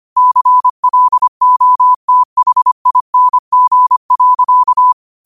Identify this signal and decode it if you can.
Morse code, heard as a keyed tone, about 25 wpm, 1000 Hz. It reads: CLOTHING.
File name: qotdfiles / 25wpm